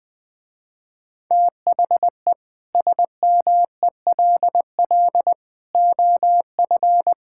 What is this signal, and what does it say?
Morse code, heard as a keyed tone, about 19 wpm, 700 Hz. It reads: THE SMELL OF
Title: Difficult Sentences with Hints - 20wpm